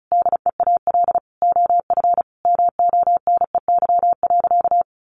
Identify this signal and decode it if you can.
Morse code, heard as a keyed tone, about 34 wpm, 700 Hz. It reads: DEAL OF MONEY.
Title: Sentences from Top 1300 Words 35wpm